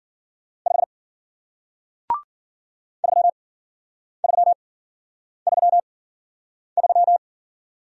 Morse code, heard as a keyed tone, about 50 wpm, spaced as for 9 wpm, 700 Hz. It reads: 5 3333
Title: Speed-Racing Single Letter-Number 40wpm